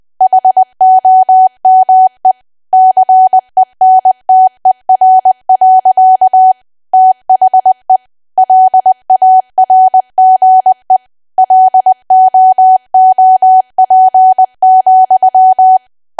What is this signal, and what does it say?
Morse code, heard as a keyed tone, about 20 wpm, 750 Hz. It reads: HOME CENTER. THE LARGE LOOP,
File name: cwa / W1AW-Level3 / 20WPM